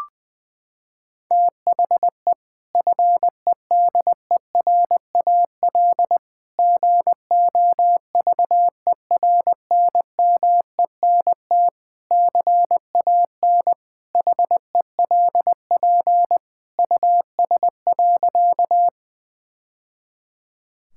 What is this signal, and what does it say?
Morse code, heard as a keyed tone, about 20 wpm, 700 Hz. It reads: THE FEDERAL GOVERNMENT CAN HELP US.
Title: Sentences from Top 1000 Words - Review Part 5 - 20wpm